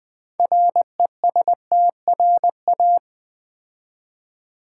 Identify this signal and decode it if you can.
Morse code, heard as a keyed tone, about 20 wpm, 700 Hz. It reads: RESTRA
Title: Build A Word - 3 Letters At A Time - 9-Letter Words - 20wpm